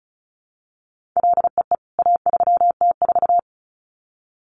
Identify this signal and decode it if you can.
Morse code, heard as a keyed tone, about 34 wpm, 700 Hz. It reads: LEE A3T4
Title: CWT Exchange 35wpm